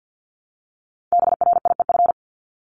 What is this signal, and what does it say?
Morse code, heard as a keyed tone, about 50 wpm, 700 Hz. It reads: BRIEF
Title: LY Suffix - Top 100 Words with Base Words 50wpm